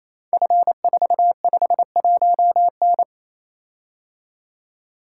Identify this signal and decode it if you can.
Morse code, heard as a keyed tone, about 27 wpm, 700 Hz. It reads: F451N
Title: Code Groups - 5 Letter:Numbers - 28wpm